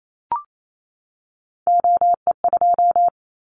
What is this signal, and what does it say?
Morse code, heard as a keyed tone, about 28 wpm, 700 Hz. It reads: OE2
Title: Callsign Prefixes for Top 25 DXCC - 28wpm